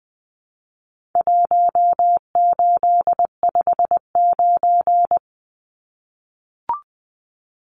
Morse code, heard as a keyed tone, about 20 wpm, 700 Hz. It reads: 1859
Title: Numbers - 4-Digit 20wpm